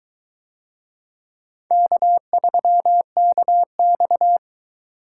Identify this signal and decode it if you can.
Morse code, heard as a keyed tone, about 21 wpm, 700 Hz. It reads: K3KX